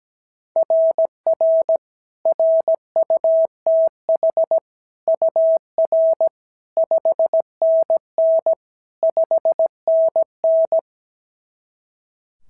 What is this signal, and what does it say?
Morse code, heard as a keyed tone, about 17 wpm, 660 Hz. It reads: RR RUTH UR 5NN 5NN
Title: SOTA QSO Practice 17wpm